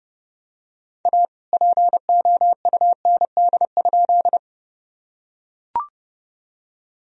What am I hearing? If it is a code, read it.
Morse code, heard as a keyed tone, about 30 wpm, 700 Hz. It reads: A POUND?
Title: Sentences from Top 1000 Words - Review Part 2 - 30wpm